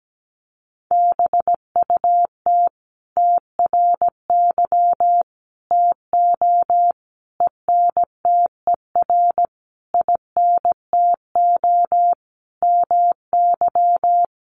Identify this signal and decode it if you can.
Morse code, heard as a keyed tone, about 17 wpm, 700 Hz. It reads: BUT TRY TO ENTER INTO MY
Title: Sentences from Top 1300 Words 17wpm